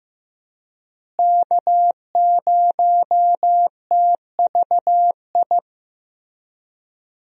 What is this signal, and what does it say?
Morse code, heard as a keyed tone, about 15 wpm, 700 Hz. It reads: K0TVI
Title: USA Call Signs - 5 Characters - Repeated 3x - 15wpm